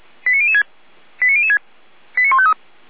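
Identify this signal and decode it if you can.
Five-tone selective call (ZVEI), heard as quick runs of five tones, tones of about 70 ms. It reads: 89006 89006 89153
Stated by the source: Five-tone ZVEI selective call 89150, received by SDR.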